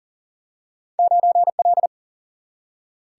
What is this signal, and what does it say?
Morse code, heard as a keyed tone, about 40 wpm, 700 Hz. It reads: 9L
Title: Single-Letter Single-Number 40wpm